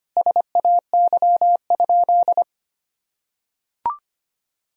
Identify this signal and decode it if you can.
Morse code, heard as a keyed tone, about 25 wpm, 700 Hz. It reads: SAY?
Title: Sentences from Top 1000 Words - Review Part 1 - 25wpm